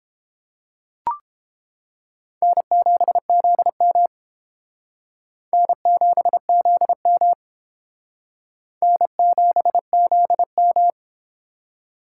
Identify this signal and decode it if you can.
Morse code, heard as a keyed tone, about 31 wpm, 700 Hz. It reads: N7ZM N7ZM N7ZM